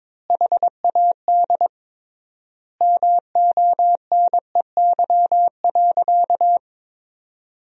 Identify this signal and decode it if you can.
Morse code, heard as a keyed tone, about 22 wpm, 700 Hz. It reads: HAD MONEY.
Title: 3x Word-Spacing - Sentences from Top 500 Words - 22wpm